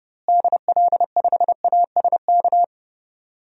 Morse code, heard as a keyed tone, about 31 wpm, 700 Hz. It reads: DL5ASK
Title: DXCC-100 Call Signs - 6 Characters - Repeated 3x - 30wpm